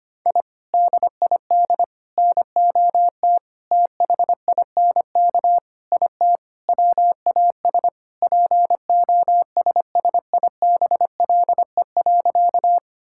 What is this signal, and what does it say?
Morse code, heard as a keyed tone, about 25 wpm, 700 Hz. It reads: I DID NOT THINK IT WAS POSSIBLE.
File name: practice / 25wpm / Sentences with 4-Word Phrases - 25wpm